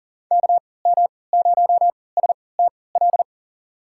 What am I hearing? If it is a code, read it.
Morse code, heard as a keyed tone, about 40 wpm, spaced as for 23 wpm, 700 Hz. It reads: KM0STL
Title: USA Call Signs - 6 Characters - 3x Character Spacing - 40wpm